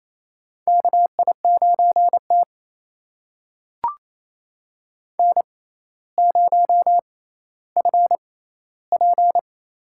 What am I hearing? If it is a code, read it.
Morse code, heard as a keyed tone, about 28 wpm, spaced as for 14 wpm, 700 Hz. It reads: KI9T N0FP
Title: USA Call Signs - 4 Characters - 6x Character Spacing - 28wpm